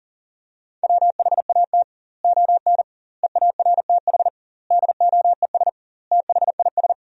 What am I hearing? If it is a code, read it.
Morse code, heard as a keyed tone, about 40 wpm, 700 Hz. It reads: WHAT ON EARTH DOES THIS
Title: 2x Word-Spacing - Sentences from Top 200 Words - 40wpm